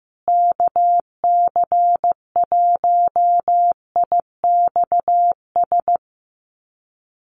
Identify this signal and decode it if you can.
Morse code, heard as a keyed tone, about 15 wpm, 700 Hz. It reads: KC1IXS
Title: USA Call Signs Repeated 3x 15wpm